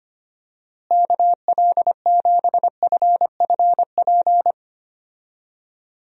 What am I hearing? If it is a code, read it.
Morse code, heard as a keyed tone, about 25 wpm, 700 Hz. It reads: KL7FFP
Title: USA Call Signs - 6 Characters - Repeated 3x - 25wpm